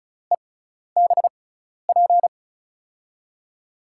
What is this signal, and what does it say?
Morse code, heard as a keyed tone, about 36 wpm, spaced as for 12 wpm, 700 Hz. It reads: EBP
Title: USA Call Signs - 6x Character Spacing - 35wpm